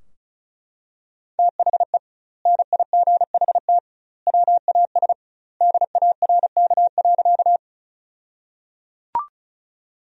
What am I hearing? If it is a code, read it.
Morse code, heard as a keyed tone, about 36 wpm, 700 Hz. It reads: THE NIGHT WAS DARK.
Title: 2x Word-Spacing - Sentences from Top 500 Words - 35wpm